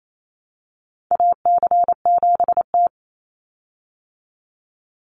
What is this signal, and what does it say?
Morse code, heard as a keyed tone, about 28 wpm, 700 Hz. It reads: AC7T
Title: Call Signs Repeated 2x - 28wpm